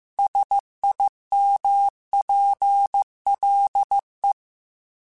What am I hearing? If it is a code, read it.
Morse code, heard as a keyed tone, about 15 wpm, 800 Hz. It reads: SIMPLE
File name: Technical Subjects / Technical Subjects-mp3s / AND Gate @13